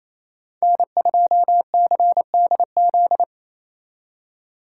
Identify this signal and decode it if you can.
Morse code, heard as a keyed tone, about 36 wpm, 700 Hz. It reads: N2CDZ